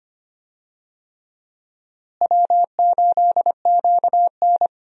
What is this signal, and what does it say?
Morse code, heard as a keyed tone, about 25 wpm, 700 Hz. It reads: W8QN
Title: USA Call Signs - 4 Characters - Repeated 3x - 25wpm